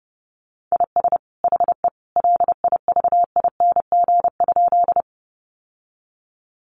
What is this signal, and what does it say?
Morse code, heard as a keyed tone, about 30 wpm, 700 Hz. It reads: IS HE LIVING?
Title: Sentences from Top 1100 Words 30wpm